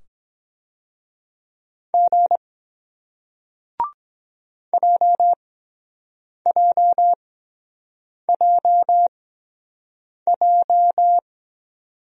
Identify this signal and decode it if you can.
Morse code, heard as a keyed tone, about 22 wpm, spaced as for 7 wpm, 700 Hz. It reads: G JJJJ